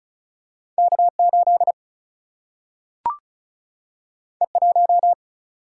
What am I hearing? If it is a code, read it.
Morse code, heard as a keyed tone, about 36 wpm, 700 Hz. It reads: K 8 E 1
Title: Single-Letter Single-Number 35wpm